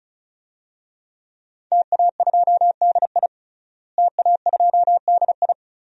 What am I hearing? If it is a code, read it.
Morse code, heard as a keyed tone, about 36 wpm, 700 Hz. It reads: TA2DI TA2DI
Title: DXCC-100 Call Signs - 5 Characters - Repeated 2x - 35wpm